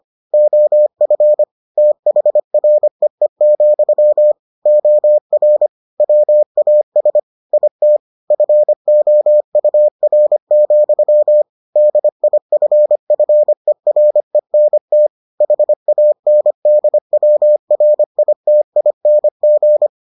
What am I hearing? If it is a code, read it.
Morse code, heard as a keyed tone, about 25 wpm, 600 Hz. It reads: OF THREE, OR WAS IT FOUR, DIFFERENT HANDWRITING